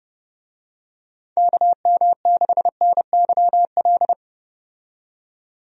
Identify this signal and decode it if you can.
Morse code, heard as a keyed tone, about 32 wpm, 700 Hz. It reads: KM6NYL
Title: USA Call Signs - 6 Characters - 30wpm